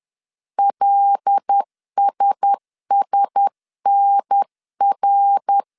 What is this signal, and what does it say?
Morse code, heard as a keyed tone, about 10 wpm, 800 Hz. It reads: LSSNR
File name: CW / DET / 1 6 1500C S C 10wpm 4252